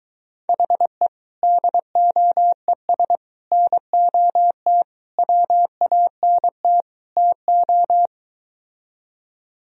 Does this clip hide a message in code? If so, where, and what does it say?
Yes: Morse code from 0.5 to 8.1 s, heard as a keyed tone, about 23 wpm, 700 Hz. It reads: HE DOES NOT WANT TO